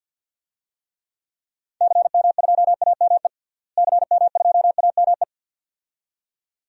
Morse code, heard as a keyed tone, about 52 wpm, 700 Hz. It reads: CM2AME CM2AME
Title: DXCC-100 Call Signs - 6 Characters - Repeated 2x - 50wpm